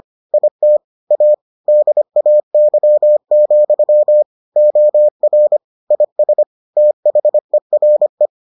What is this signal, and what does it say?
Morse code, heard as a keyed tone, about 25 wpm, 600 Hz. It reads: IT A DAY, OR IS THERE